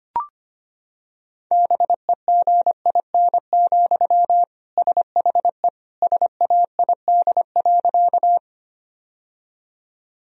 Morse code, heard as a keyed tone, about 25 wpm, 700 Hz. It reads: BEGIN, SHE SAID.
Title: Sentences from Top 1000 Words - Review Part 1 - 25wpm